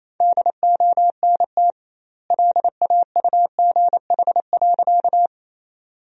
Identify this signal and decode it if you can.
Morse code, heard as a keyed tone, about 27 wpm, 700 Hz. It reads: DONT LAUGH.
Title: 2x Word-Spacing - Sentences from Top 500 Words - 28wpm